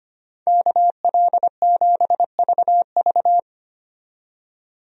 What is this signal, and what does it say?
Morse code, heard as a keyed tone, about 25 wpm, 700 Hz. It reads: KL7VV
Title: USA Call Signs - 5 Characters - 25wpm